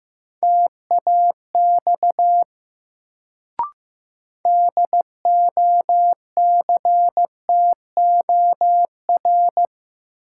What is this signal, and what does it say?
Morse code, heard as a keyed tone, about 15 wpm, 700 Hz. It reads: TAX DOCTOR
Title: Sentences from Top 1500 Words - New Word Drill Alt - 15wpm